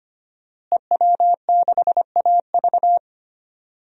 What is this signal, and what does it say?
Morse code, heard as a keyed tone, about 25 wpm, 700 Hz. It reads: EW6AV
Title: DXCC-100 Call Signs - 5 Characters - Repeated 2x - 25wpm